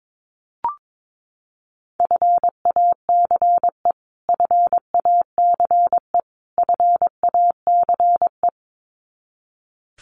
Morse code, heard as a keyed tone, about 22 wpm, 700 Hz. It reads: FACE FACE FACE
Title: Top 300 Words Repeated 22wpm